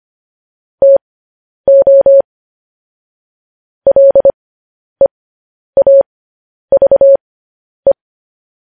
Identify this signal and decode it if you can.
Morse code, heard as a keyed tone, about 25 wpm, spaced as for 10 wpm, 550 Hz. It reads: TO LEAVE
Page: News Headlines in Morse Code at 10 WPM